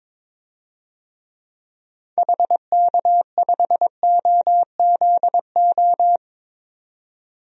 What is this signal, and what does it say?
Morse code, heard as a keyed tone, about 21 wpm, 700 Hz. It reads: HK5OZO
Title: DXCC-100 Call Signs - 6 Characters - 22wpm